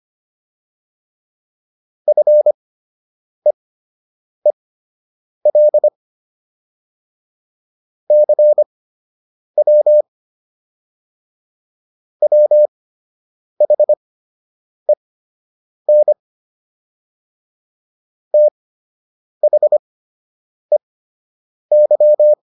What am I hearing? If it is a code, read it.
Morse code, heard as a keyed tone, about 25 wpm, spaced as for 8 wpm, 600 Hz. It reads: FEEL CW WHEN THEY